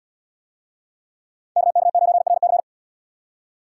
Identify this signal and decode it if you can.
Morse code, heard as a keyed tone, about 71 wpm, 700 Hz. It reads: KK9RC